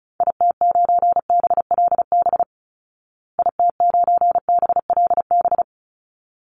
Morse code, heard as a keyed tone, about 34 wpm, 700 Hz. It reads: IT9BLB IT9BLB
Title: Call Signs Repeated 3x - 35wpm